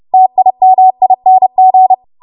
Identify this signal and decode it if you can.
Morse code, heard as a keyed tone, about 30 wpm, 750 Hz. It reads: TIMING
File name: Morse / 30WPM